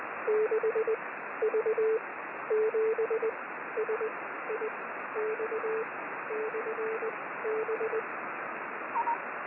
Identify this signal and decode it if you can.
Morse code, heard as a keyed tone, about 20 wpm, spaced as for 13 wpm, 440 Hz, with radio noise. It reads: 6V7SIX/B
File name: 6V7SIX B